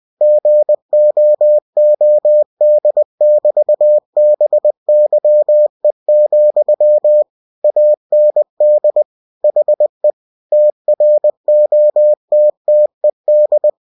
Morse code, heard as a keyed tone, about 20 wpm, 600 Hz. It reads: GOOD<BT>BYE, AND HE TROTTED
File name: cw-books / winnie-the-pooh / winnie-20wpm-page-0024